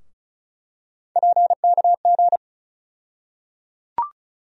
Morse code, heard as a keyed tone, about 34 wpm, 700 Hz. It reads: PKG
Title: USA Call Sign Postfixes - 3 Letters 35wpm